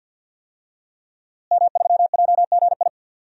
Morse code, heard as a keyed tone, about 49 wpm, 700 Hz. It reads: M3JGI
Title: DXCC-100 Call Signs - 5 Characters - Repeated 2x - 50wpm